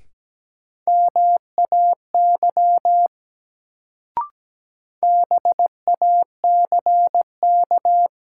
Morse code, heard as a keyed tone, about 17 wpm, 700 Hz. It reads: MAY BACK
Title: Sets of 2 Words - New Word Drill - 17wpm